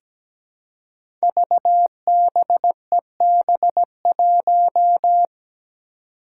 Morse code, heard as a keyed tone, about 17 wpm, 700 Hz. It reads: VBEB1
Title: Five Challenging Letter-Number Combinations 17wpm